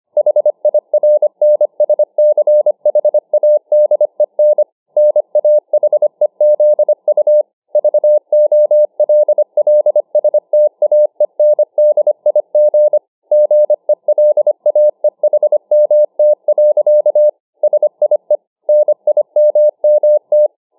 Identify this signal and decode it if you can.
Morse code, heard as a keyed tone, about 25 wpm, 600 Hz. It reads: HIRNSCHADEN NAHEZU VOLLSTAENDIG GELAEHMT. SIE NIMMT